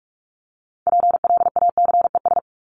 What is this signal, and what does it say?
Morse code, heard as a keyed tone, about 44 wpm, 700 Hz. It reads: PLACES
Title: ES:S Suffix - Top 100 Words with Base Words 45wpm